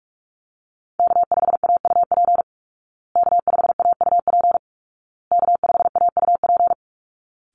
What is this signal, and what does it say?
Morse code, heard as a keyed tone, about 44 wpm, 700 Hz. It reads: K5AUP K5AUP K5AUP
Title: Call Signs Repeated 2x - 45wpm